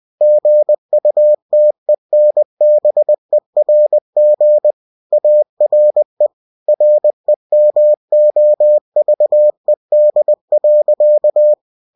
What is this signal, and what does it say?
Morse code, heard as a keyed tone, about 20 wpm, 600 Hz. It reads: GUTENBERG ARE REMOVED.
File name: cw-books / winnie-the-pooh / winnie-20wpm-page-0114